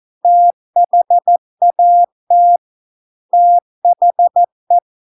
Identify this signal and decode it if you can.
Morse code, heard as a keyed tone, about 14 wpm, 700 Hz. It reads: THAT THE